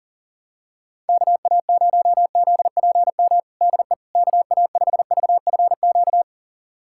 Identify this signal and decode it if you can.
Morse code, heard as a keyed tone, about 39 wpm, 700 Hz. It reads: KA0ZPM DE KA5VFQ
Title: USA Callsign DE Callsign - 6 Characters - 40wpm